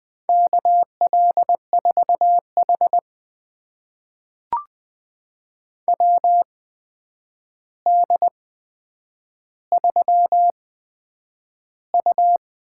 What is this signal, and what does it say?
Morse code, heard as a keyed tone, about 20 wpm, spaced as for 9 wpm, 700 Hz. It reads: KL4H WD3U
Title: USA Call Signs - 4 Characters - 8x Character Spacing - 20wpm